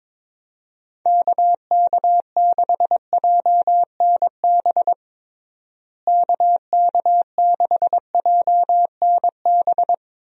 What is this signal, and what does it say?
Morse code, heard as a keyed tone, about 22 wpm, 700 Hz. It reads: KK6JNB KK6JNB
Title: USA Call Signs - 6 Characters - Repeated 3x - 22wpm